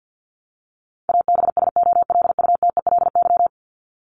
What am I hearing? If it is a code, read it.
Morse code, heard as a keyed tone, about 50 wpm, 700 Hz. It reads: ABSOLUTELY
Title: Top 1501-1750 Words 50wpm